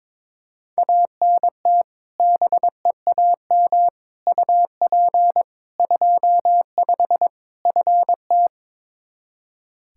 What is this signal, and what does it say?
Morse code, heard as a keyed tone, about 22 wpm, 700 Hz. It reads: ANT BEAM UP 25 FT
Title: K1USN SST QSO Practice 22wpm